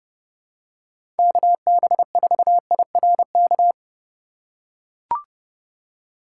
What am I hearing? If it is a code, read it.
Morse code, heard as a keyed tone, about 30 wpm, 700 Hz. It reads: KB4IRK
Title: USA Call Signs - 6 Characters - 3x Character Spacing - 30wpm